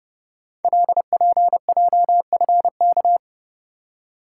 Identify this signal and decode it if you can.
Morse code, heard as a keyed tone, about 30 wpm, 700 Hz. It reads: LPJFK
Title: Code Groups - 5 Letters - 30wpm